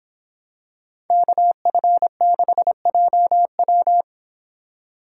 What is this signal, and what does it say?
Morse code, heard as a keyed tone, about 26 wpm, 700 Hz. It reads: KF6JW